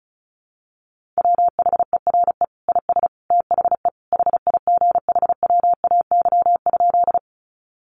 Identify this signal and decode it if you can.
Morse code, heard as a keyed tone, about 34 wpm, 700 Hz. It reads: WHERE IS THE HIGHWAY?
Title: Sentences from Top 1400 Words 35wpm